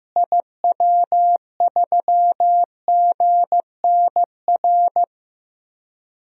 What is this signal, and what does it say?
Morse code, heard as a keyed tone, about 16 wpm, 700 Hz. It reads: IW3GNR
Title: DXCC-100 Call Signs - 6 Characters - Repeated 2x - 15wpm